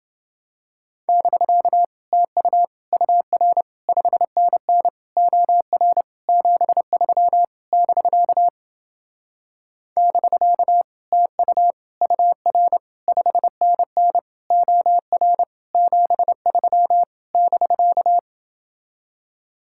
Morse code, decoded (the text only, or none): <BK> TU UR 5NN OR 73 <BK> <BK> TU UR 5NN OR 73 <BK>